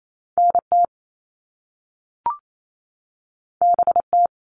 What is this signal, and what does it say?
Morse code, heard as a keyed tone, about 28 wpm, 700 Hz. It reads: NT BT
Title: All Cut Numbers - 2-Digit 28wpm